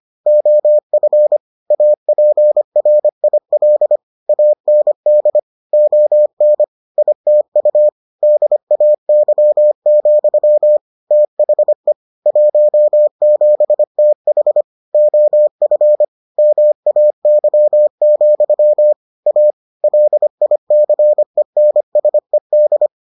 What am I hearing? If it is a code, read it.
Morse code, heard as a keyed tone, about 25 wpm, 600 Hz. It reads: OF APRIL AND ON ITU DAY, THE 17TH OF MAY, A LICENSED